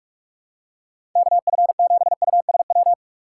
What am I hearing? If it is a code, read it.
Morse code, heard as a keyed tone, about 43 wpm, 700 Hz. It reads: KF7USW
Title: USA Call Signs - 6 Characters - 2x Character Spacing - 45wpm